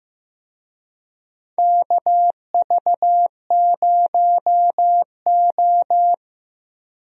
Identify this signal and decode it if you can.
Morse code, heard as a keyed tone, about 15 wpm, 700 Hz. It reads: KV0O
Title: USA Call Signs - 4 Characters - Repeated 2x - 15wpm